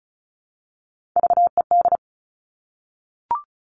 Morse code, heard as a keyed tone, about 36 wpm, 700 Hz. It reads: VED
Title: Top Three Letter Combinations in Words 35wpm